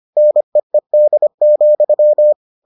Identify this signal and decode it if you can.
Morse code, heard as a keyed tone, about 25 wpm, 600 Hz. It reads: NEED,